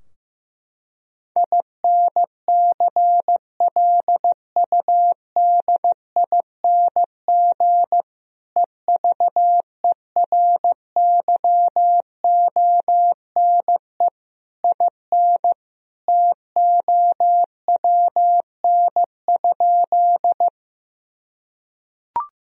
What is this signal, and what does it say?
Morse code, heard as a keyed tone, about 16 wpm, 700 Hz. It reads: INCLUDING EVERYONE IN TOWN?
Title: Sentences from Top 1000 Words - Review Part 4 - 15wpm